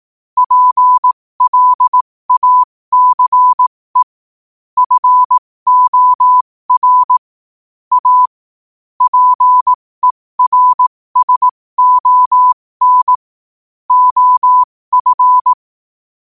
Morse code, heard as a keyed tone, about 18 wpm, spaced as for 15 wpm, 1000 Hz. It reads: PLACE FOR A PERSON OF
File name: qotdfiles / 15wpm